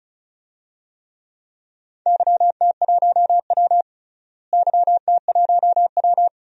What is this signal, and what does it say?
Morse code, heard as a keyed tone, about 34 wpm, 700 Hz. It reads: YT1W YT1W
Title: DXCC-100 Call Signs - 4 Characters - Repeated 2x - 35wpm